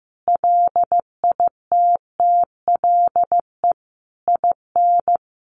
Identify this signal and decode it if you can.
Morse code, heard as a keyed tone, about 15 wpm, 700 Hz. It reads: LITTLE IN
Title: Sentences from Top 1400 Words 15wpm